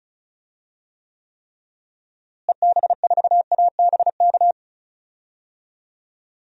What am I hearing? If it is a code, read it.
Morse code, heard as a keyed tone, about 34 wpm, 700 Hz. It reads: EB4ABK
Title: DXCC-100 Call Signs - 6 Characters - Repeated 3x - 35wpm